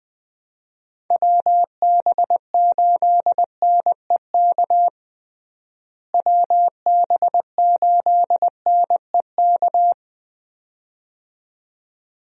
Morse code, heard as a keyed tone, about 21 wpm, 700 Hz. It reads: WB8NEK WB8NEK
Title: USA Call Signs - 6 Characters - Repeated 2x - 20wpm